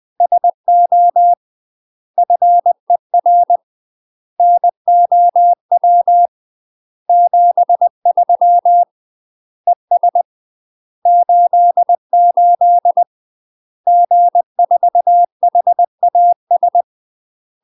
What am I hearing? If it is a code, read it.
Morse code, heard as a keyed tone, about 20 wpm, 700 Hz. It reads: SO FER NOW 73 ES 88 G4HAS
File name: UK MASTER QSOs TRAINING / 20WPM / UK MASTER QSOs 0030